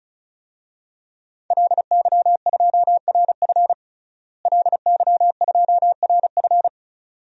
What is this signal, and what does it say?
Morse code, heard as a keyed tone, about 34 wpm, 700 Hz. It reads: LY2RF LY2RF
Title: DXCC-100 Call Signs - 5 Characters - Repeated 2x - 35wpm